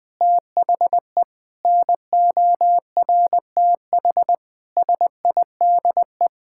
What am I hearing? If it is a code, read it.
Morse code, heard as a keyed tone, about 20 wpm, 700 Hz. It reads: THE NORTH SIDE
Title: Sentences with 6-Word Phrases - 20wpm